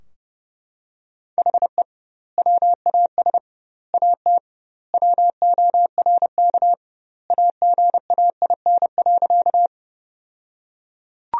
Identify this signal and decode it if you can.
Morse code, heard as a keyed tone, about 32 wpm, 700 Hz. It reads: HE WAS AT WORK AGAIN.
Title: 2x Word-Spacing - Sentences from Top 200 Words - 30wpm